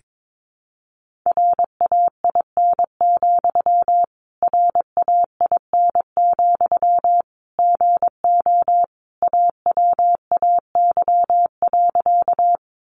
Morse code, heard as a keyed tone, about 22 wpm, 700 Hz. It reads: RAIN, RAIN, GO AWAY.
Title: Sentences from Top 600 Words 22wpm